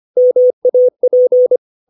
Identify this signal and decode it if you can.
Morse code, heard as a keyed tone, about 25 wpm, 500 Hz. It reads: MAP
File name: 632 words / 632 Words 25WPM